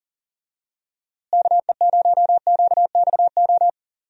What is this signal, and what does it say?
Morse code, heard as a keyed tone, about 38 wpm, 700 Hz. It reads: KE0QXO
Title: USA Call Signs - 6 Characters - Repeated 3x - 40wpm